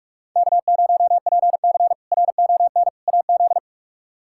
Morse code, heard as a keyed tone, about 42 wpm, 700 Hz. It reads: K0PC RON AZ
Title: K1USN SST QSO Practice - 45wpm